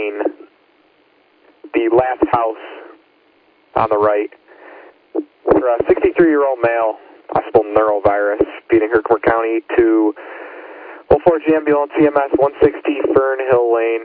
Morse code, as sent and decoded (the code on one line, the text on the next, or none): none